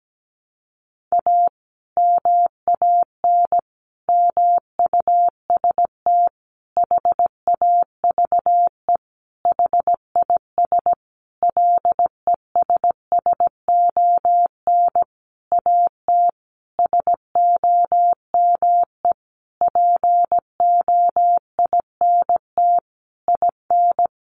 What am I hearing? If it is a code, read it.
Morse code, heard as a keyed tone, about 17 wpm, 700 Hz. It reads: A MAN MUST HAVE HIS LESSON AT SOME POINT IN
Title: Sentences from Top 1200 Words 17wpm